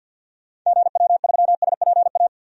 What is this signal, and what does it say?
Morse code, heard as a keyed tone, about 46 wpm, 700 Hz. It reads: GW3SPA
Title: DXCC-100 Call Signs - 6 Characters - 50wpm